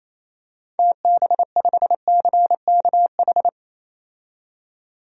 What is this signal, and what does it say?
Morse code, heard as a keyed tone, about 28 wpm, 700 Hz. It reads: TB5CKH
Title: DXCC-100 Call Signs - 6 Characters - 28wpm